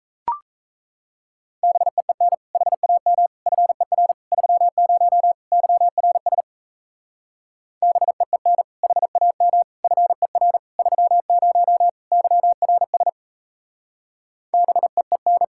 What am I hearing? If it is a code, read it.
Morse code, heard as a keyed tone, about 36 wpm, 700 Hz. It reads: BEEN HAM FER 30 YRS BEEN HAM FER 30 YRS BEEN